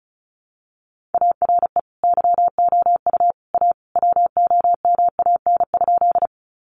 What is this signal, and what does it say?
Morse code, heard as a keyed tone, about 34 wpm, 700 Hz. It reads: ARE YOU A WOMAN?
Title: Sentences from Top 1100 Words 35wpm